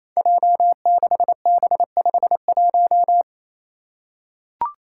Morse code, heard as a keyed tone, about 28 wpm, 700 Hz. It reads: J6B51
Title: Five Challenging Letter-Number Combinations 28wpm